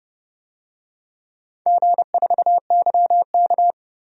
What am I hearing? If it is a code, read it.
Morse code, heard as a keyed tone, about 30 wpm, 700 Hz. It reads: G4YK
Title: DXCC-100 Call Signs - 4 Characters - 30wpm